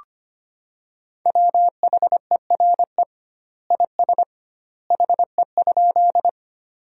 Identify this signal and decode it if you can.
Morse code, heard as a keyed tone, about 25 wpm, 700 Hz. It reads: WHERE IS HE?
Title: 2x Word-Spacing - Sentences from Top 200 Words - 25wpm